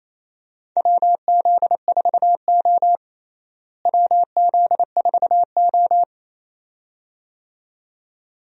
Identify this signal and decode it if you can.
Morse code, heard as a keyed tone, about 28 wpm, 700 Hz. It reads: WZ4O WZ4O
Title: USA Call Signs - 4 Characters - Repeated 2x - 28wpm